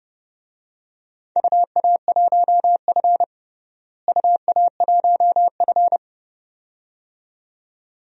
Morse code, heard as a keyed tone, about 30 wpm, 700 Hz. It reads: UA1F UA1F
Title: DXCC-100 Call Signs - 4 Characters - Repeated 2x - 30wpm